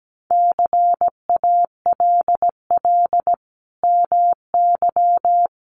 Morse code, heard as a keyed tone, about 17 wpm, 700 Hz. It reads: CALL MY
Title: Sentences from Top 1500 Words 17wpm